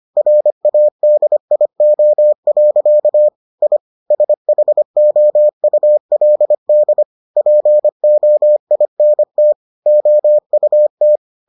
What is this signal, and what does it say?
Morse code, heard as a keyed tone, about 25 wpm, 600 Hz. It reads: RADIO. I SHOULD POINT OUT